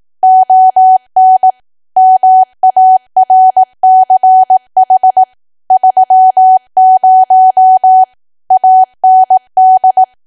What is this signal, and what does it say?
Morse code, heard as a keyed tone, about 18 wpm, 750 Hz. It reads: ON MARCH 30 AND
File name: cwa / W1AW-Level2 / 18WPM